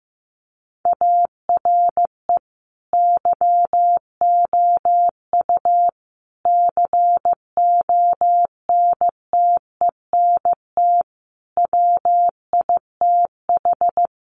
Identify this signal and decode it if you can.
Morse code, heard as a keyed tone, about 15 wpm, 700 Hz. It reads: ARE YOU CONTENT WITH
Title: Sentences from Top 900 Words 15wpm